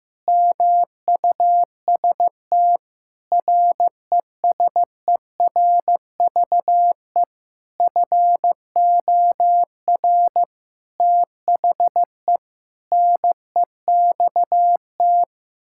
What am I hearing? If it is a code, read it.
Morse code, heard as a keyed tone, about 16 wpm, 700 Hz. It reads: MUST RESERVE FOR THE NEXT
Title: Sentences from Top 1000 Words - Review Part 6 - 15wpm